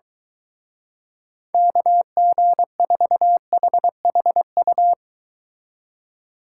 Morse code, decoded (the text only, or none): KG4HHU